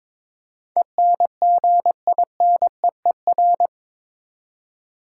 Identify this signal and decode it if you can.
Morse code, heard as a keyed tone, about 22 wpm, 700 Hz. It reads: ENGINEER
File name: practice / 22wpm / Eight Letter Words - 22wpm